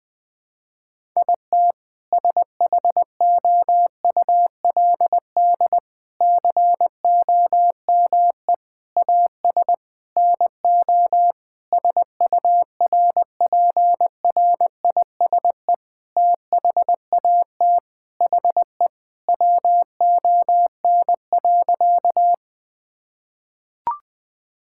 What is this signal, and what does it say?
Morse code, heard as a keyed tone, about 20 wpm, 700 Hz. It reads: IT SHOULD COME AS NO SURPRISE THAT HE WON.
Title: Sentences with 6-Word Phrases - 20wpm